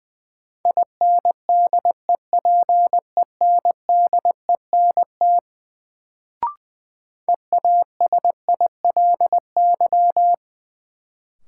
Morse code, heard as a keyed tone, about 21 wpm, 700 Hz. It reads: INDEPENDENT EASILY
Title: Sentences from Top 900 Words - New Word Drill - 20wpm